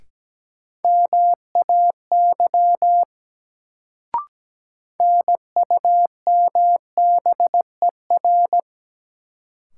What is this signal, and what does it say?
Morse code, heard as a keyed tone, about 17 wpm, 700 Hz. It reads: MAY NUMBER
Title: Sentences from Top 100 Words - New Word Drill - 17wpm